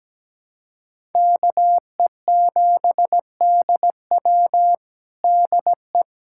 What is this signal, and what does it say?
Morse code, heard as a keyed tone, about 17 wpm, 700 Hz. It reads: KE7DW DE
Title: USA Callsign DE Callsign - 5 Characters - 17wpm